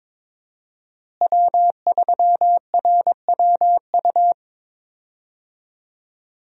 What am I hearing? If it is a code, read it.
Morse code, heard as a keyed tone, about 22 wpm, 700 Hz. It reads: W3RWU
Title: USA Call Signs - 5 Characters - Repeated 3x - 22wpm